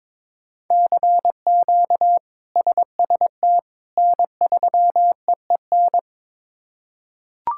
Morse code, heard as a keyed tone, about 22 wpm, 700 Hz. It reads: CQ SST N3EEN
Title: K1USN SST QSO Practice - 22wpm